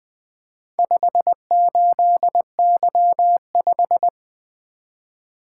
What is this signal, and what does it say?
Morse code, heard as a keyed tone, about 20 wpm, 700 Hz. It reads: 58Y5
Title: Four Letter-Number Combinations - 20wpm